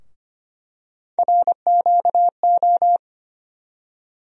Morse code, heard as a keyed tone, about 25 wpm, 700 Hz. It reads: RQO
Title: USA Call Sign Postfixes - 3 Letters 25wpm